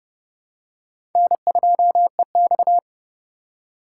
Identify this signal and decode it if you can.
Morse code, heard as a keyed tone, about 31 wpm, 700 Hz. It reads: N2EX
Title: USA Call Signs - 4 Characters - 4x Character Spacing - 30wpm